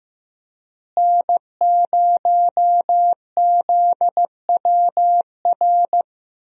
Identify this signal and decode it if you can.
Morse code, heard as a keyed tone, about 15 wpm, 700 Hz. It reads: N0ZWR
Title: USA Call Signs - 5 Characters - Repeated 3x - 15wpm